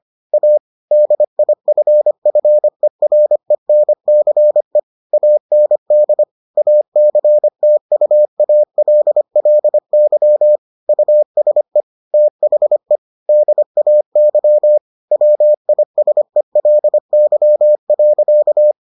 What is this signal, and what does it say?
Morse code, heard as a keyed tone, about 25 wpm, 600 Hz. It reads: A DIFFERENCE AND ACTUALLY USE THE DAY WISELY.